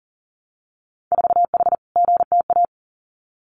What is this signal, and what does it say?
Morse code, heard as a keyed tone, about 40 wpm, 700 Hz. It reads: 4H GTA
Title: Winter Field Day Exchanges Encore 40wpm